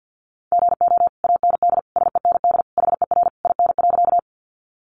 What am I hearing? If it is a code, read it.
Morse code, heard as a keyed tone, about 50 wpm, 700 Hz. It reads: GO AND SEND HER IN.
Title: Sentences from Top 1500 Words 50wpm